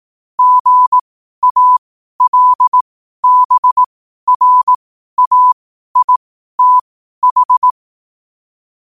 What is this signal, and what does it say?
Morse code, heard as a keyed tone, about 18 wpm, spaced as for 12 wpm, 1000 Hz. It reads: GALBRAITH
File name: qotdfiles / 12wpm